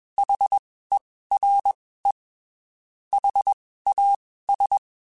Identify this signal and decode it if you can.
Morse code, heard as a keyed tone, about 21 wpm, spaced as for 15 wpm, 800 Hz. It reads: HERE HAS
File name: (spaced) / QSOs Spaced-mp3s / WD0FBT @20 sp